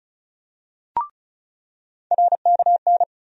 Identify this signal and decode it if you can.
Morse code, heard as a keyed tone, about 34 wpm, 700 Hz. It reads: RKN
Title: USA Call Sign Postfixes - 3 Letters 35wpm